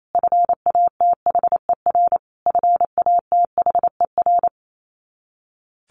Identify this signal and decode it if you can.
Morse code, heard as a keyed tone, about 28 wpm, 700 Hz. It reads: FATHER FATHER
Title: Top 300 Words Repeated 28wpm